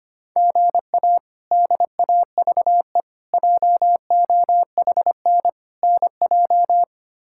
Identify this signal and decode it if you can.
Morse code, heard as a keyed tone, about 25 wpm, 700 Hz. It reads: GA DAVE JOHN NJ
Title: K1USN SST QSO Practice - 25wpm